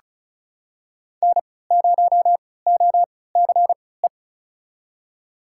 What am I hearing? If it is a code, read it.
Morse code, heard as a keyed tone, about 36 wpm, spaced as for 20 wpm, 700 Hz. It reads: N0OCE
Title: USA Call Signs - 3x Character Spacing - 35wpm